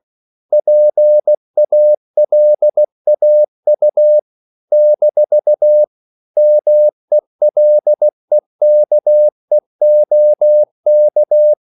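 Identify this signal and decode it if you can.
Morse code, heard as a keyed tone, about 16 wpm, 600 Hz. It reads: PALAU - MELEKEOK
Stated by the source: CW practice of national capitals at 16 words per minute, from ON6ZQ.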